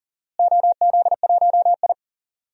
Morse code, heard as a keyed tone, about 40 wpm, 700 Hz. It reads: OZ1I
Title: DXCC-100 Call Signs - 4 Characters - Repeated 2x - 40wpm